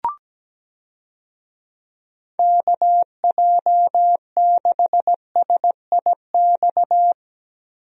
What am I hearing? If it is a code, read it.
Morse code, heard as a keyed tone, about 17 wpm, 700 Hz. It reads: KJ6SIX
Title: USA Call Signs - 6 Characters - Repeated 2x - 17wpm